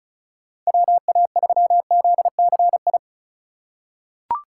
Morse code, heard as a keyed tone, about 36 wpm, 700 Hz. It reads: WA3ZCI
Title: USA Call Signs - 6 Characters - 4x Character Spacing - 35wpm